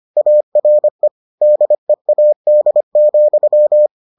Morse code, heard as a keyed tone, about 25 wpm, 600 Hz. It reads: ARE DEAD,